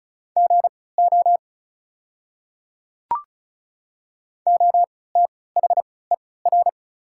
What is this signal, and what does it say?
Morse code, heard as a keyed tone, about 36 wpm, spaced as for 20 wpm, 690 Hz. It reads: GO OTHER
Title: 3x Character Spacing - Top 100 Words - 35wpm